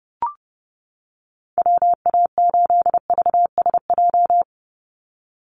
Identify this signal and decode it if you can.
Morse code, heard as a keyed tone, about 30 wpm, 700 Hz. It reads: WA8VSJ
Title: Call Signs - 30wpm